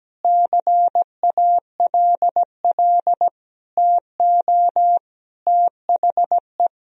Morse code, heard as a keyed tone, about 17 wpm, 700 Hz. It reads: CALL TO THE
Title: Sentences from Top 1000 Words - Review Part 2 - 17wpm